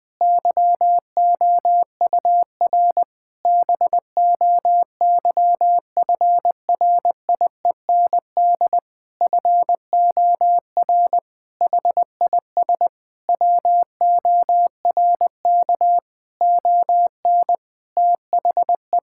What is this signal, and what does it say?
Morse code, heard as a keyed tone, about 21 wpm, 700 Hz. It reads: YOUR BOYFRIEND FOR HIS WORK ON THE
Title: Sentences from Top 1000 Words - Review Part 5 - 20wpm